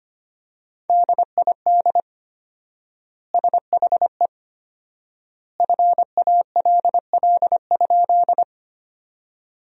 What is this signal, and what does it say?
Morse code, heard as a keyed tone, about 25 wpm, 700 Hz. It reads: DID SHE FALL?
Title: 4x Word-Spacing - Sentences from Top 500 Words - 25wpm